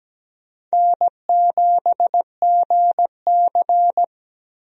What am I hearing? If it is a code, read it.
Morse code, heard as a keyed tone, about 17 wpm, 700 Hz. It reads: N7GC
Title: USA Call Signs - 4 Characters - Repeated 2x - 17wpm